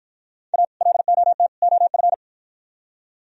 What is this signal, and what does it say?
Morse code, heard as a keyed tone, about 52 wpm, 700 Hz. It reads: A LOT OF